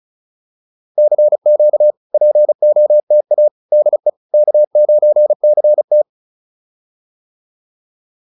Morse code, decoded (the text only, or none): CQ POTA DE K9CT